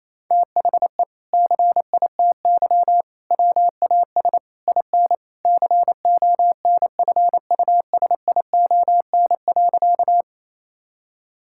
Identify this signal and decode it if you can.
Morse code, heard as a keyed tone, about 28 wpm, 700 Hz. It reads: THE CITY WAS IN CONFUSION.
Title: Sentences from Top 1600 Words 28wpm